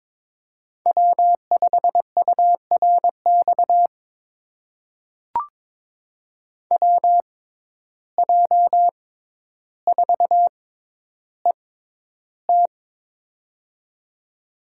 Morse code, heard as a keyed tone, about 23 wpm, spaced as for 11 wpm, 700 Hz. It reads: W5URX WJ4ET
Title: USA Call Signs - 5 Characters - 6x Character Spacing - 22wpm